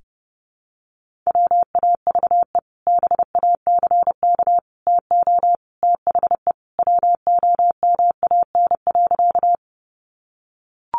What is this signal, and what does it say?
Morse code, heard as a keyed tone, about 30 wpm, 700 Hz. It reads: WAVE BACK TO THE WOMAN.
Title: Sentences from Top 1400 Words 30wpm